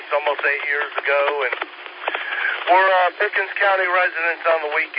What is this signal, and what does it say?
Emergency Alert System (SAME) tones, heard as harsh buzzing bursts, with speech. It is heard from 0.4 to 4.4 s.